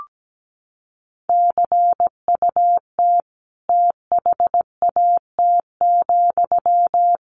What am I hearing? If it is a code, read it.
Morse code, heard as a keyed tone, about 17 wpm, 700 Hz. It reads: CUT THAT,
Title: A Sound Of Thunder 17wpm